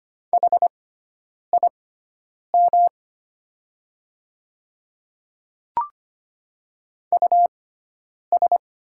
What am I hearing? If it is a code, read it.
Morse code, heard as a keyed tone, about 25 wpm, spaced as for 9 wpm, 690 Hz. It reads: HIM US
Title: 6x Character Spacing - Top 500 Words - 25wpm